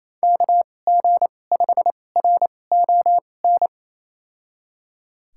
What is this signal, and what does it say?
Morse code, heard as a keyed tone, about 28 wpm, spaced as for 20 wpm, 700 Hz. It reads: KG5RON
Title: USA Call Signs - 6 Characters - 2x Character Spacing - 28wpm